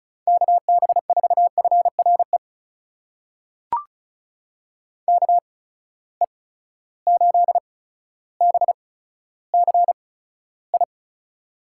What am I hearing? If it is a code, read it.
Morse code, heard as a keyed tone, about 36 wpm, spaced as for 15 wpm, 700 Hz. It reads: KB4FRE KE8BCI
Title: USA Call Signs - 6 Characters - 8x Character Spacing - 35wpm